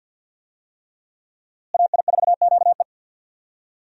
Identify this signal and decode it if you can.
Morse code, heard as a keyed tone, about 52 wpm, 700 Hz. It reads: AI4QE
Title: USA Call Signs - 5 Characters - Repeated 3x - 50wpm